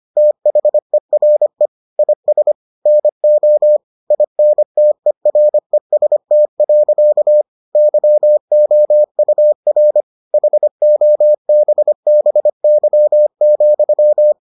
Morse code, heard as a keyed tone, about 25 wpm, 600 Hz. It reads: THERE IS NO INTEREST. YOUR HOBBY,